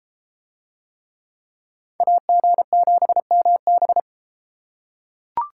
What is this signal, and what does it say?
Morse code, heard as a keyed tone, about 30 wpm, 700 Hz. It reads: AG7MB